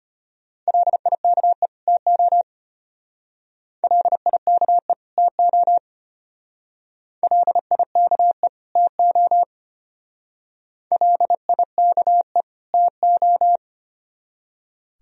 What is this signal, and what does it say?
Morse code, heard as a keyed tone, about 31 wpm, 700 Hz. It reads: LIKE TO LIKE TO LIKE TO LIKE TO
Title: Speed-Racing - Sets of 2 Words 25wpm